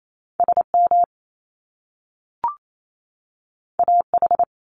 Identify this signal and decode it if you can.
Morse code, heard as a keyed tone, about 28 wpm, 700 Hz. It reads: SM AH
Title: Top Two Letter Combinations in Words 28wpm